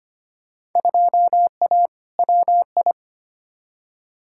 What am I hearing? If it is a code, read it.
Morse code, heard as a keyed tone, about 25 wpm, 700 Hz. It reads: 2A WI
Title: Field Day Exchanges - 25wpm